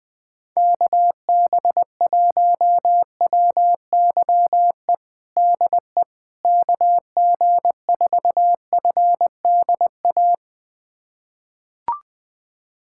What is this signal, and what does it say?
Morse code, heard as a keyed tone, about 21 wpm, 700 Hz. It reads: KB1WYE DE KG4FDA
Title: USA Callsign DE Callsign - 6 Characters - 20wpm